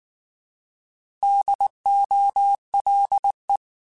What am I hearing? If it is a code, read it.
Morse code, heard as a keyed tone, about 19 wpm, 800 Hz. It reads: DOLE
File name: Alphabetical / Alphabetical mp3s / Ds @18